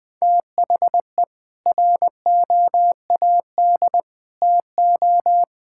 Morse code, heard as a keyed tone, about 20 wpm, 700 Hz. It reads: THE ROAD TO
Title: Sets of 8 Words - 20wpm